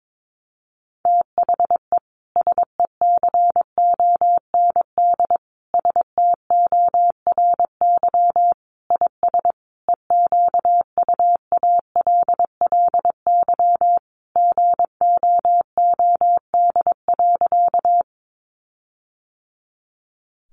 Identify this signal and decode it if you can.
Morse code, heard as a keyed tone, about 22 wpm, 700 Hz. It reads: THE SECOND STORY IS EQUALLY GOOD.
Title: Sentences from Top 1400 Words 22wpm